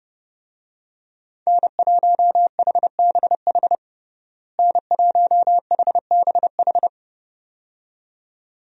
Morse code, heard as a keyed tone, about 30 wpm, 700 Hz. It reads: N1HBH N1HBH
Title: USA Call Signs - 5 Characters - Repeated 2x - 30wpm